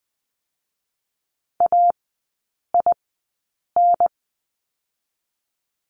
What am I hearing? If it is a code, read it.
Morse code, heard as a keyed tone, about 20 wpm, 700 Hz. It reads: A I N
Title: 029 Lesson 6e - Letter i - Three Characters 20wpm